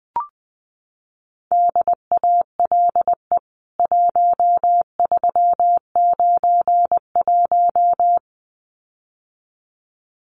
Morse code, heard as a keyed tone, about 20 wpm, 700 Hz. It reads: DALE 1391
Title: CWT Exchange 20wpm